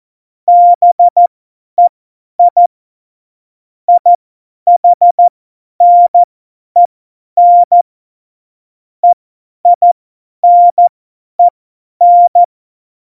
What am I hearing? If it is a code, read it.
Morse code, heard as a keyed tone, about 14 wpm, spaced as for 10 wpm, 700 Hz. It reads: BEI IHNEN EINEN
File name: Morsekurs / Witze / Witz 03. September 2019, 50 F70